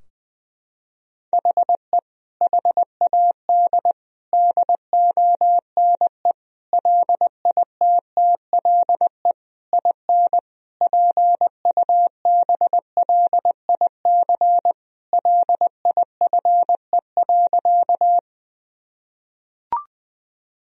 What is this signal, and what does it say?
Morse code, heard as a keyed tone, about 20 wpm, 700 Hz. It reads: HE HAD DONE LITTLE IN PUBLIC LIFE.
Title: Sentences from Top 1000 Words - Review Part 3 - 20wpm